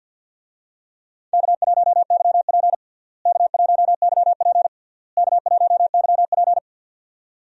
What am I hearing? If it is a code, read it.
Morse code, heard as a keyed tone, about 53 wpm, 700 Hz. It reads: K1YP K1YP K1YP
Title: USA Call Signs - 4 Characters - Repeated 3x - 50wpm